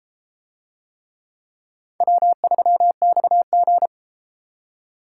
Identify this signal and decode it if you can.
Morse code, heard as a keyed tone, about 30 wpm, 700 Hz. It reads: W3XG